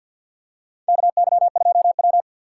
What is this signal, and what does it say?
Morse code, heard as a keyed tone, about 48 wpm, 700 Hz. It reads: KY2W
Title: USA Call Signs - 4 Characters - 3x Character Spacing - 50wpm